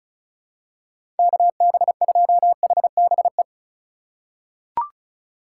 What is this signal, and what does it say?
Morse code, heard as a keyed tone, about 36 wpm, 700 Hz. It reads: KB2HBE
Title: USA Call Signs - 6 Characters - 4x Character Spacing - 35wpm